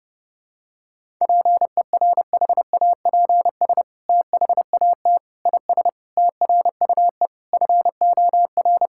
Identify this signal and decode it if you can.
Morse code, heard as a keyed tone, about 30 wpm, 700 Hz. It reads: PERHAPS THAT IS TRUE FOR
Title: Sentences from Top 1000 Words - Review Part 5 - 30wpm